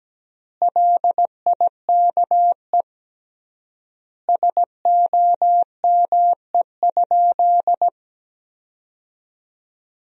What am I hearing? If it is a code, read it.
Morse code, heard as a keyed tone, about 17 wpm, 700 Hz. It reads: LIKE SOME?
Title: 3x Word-Spacing - Sentences from Top 100 Words - 17wpm